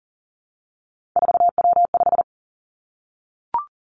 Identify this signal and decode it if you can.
Morse code, heard as a keyed tone, about 38 wpm, 700 Hz. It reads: 4W5
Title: Three Challenging Letter-Number Combinations 40wpm